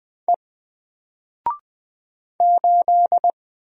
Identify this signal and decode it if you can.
Morse code, heard as a keyed tone, about 20 wpm, 700 Hz. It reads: E 8
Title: Single Letter-Number 20wpm